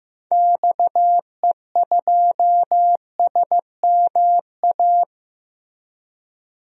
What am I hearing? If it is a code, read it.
Morse code, heard as a keyed tone, about 15 wpm, 700 Hz. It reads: XE2SMA
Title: DXCC-100 Call Signs - 6 Characters - 15wpm